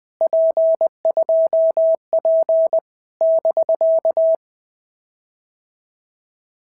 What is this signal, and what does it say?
Morse code, heard as a keyed tone, about 20 wpm, 650 Hz. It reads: P2P <BK>
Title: POTA QSO Practice - 20wpm